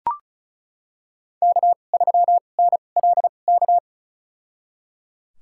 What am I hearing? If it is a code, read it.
Morse code, heard as a keyed tone, about 34 wpm, spaced as for 25 wpm, 690 Hz. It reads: K3NLK
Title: USA Call Signs - 5 Characters - 2x Character Spacing - 35wpm